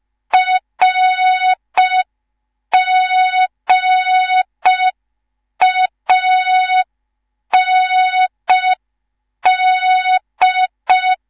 Morse code, heard as a keyed tone, about 5 wpm, 3700 Hz. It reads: RGAND
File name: Morze-HA2MN / hang / 19 GYAKORLAT